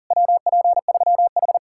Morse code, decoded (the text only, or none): WP3H